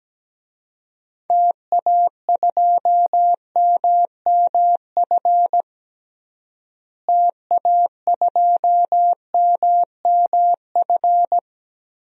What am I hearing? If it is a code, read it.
Morse code, heard as a keyed tone, about 17 wpm, 700 Hz. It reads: TA2MMF TA2MMF
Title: DXCC-100 Call Signs - 6 Characters - Repeated 3x - 17wpm